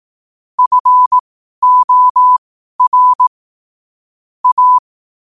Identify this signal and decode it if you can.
Morse code, heard as a keyed tone, about 18 wpm, spaced as for 12 wpm, 1000 Hz. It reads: FOR A
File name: qotdfiles / 12wpm